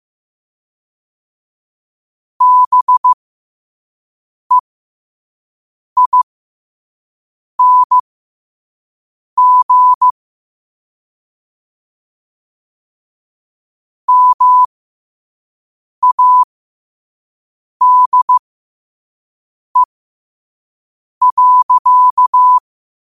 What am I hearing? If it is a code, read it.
Morse code, heard as a keyed tone, about 15 wpm, spaced as for 5 wpm, 1000 Hz. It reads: BEING MADE.
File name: qotdfiles / 5wpm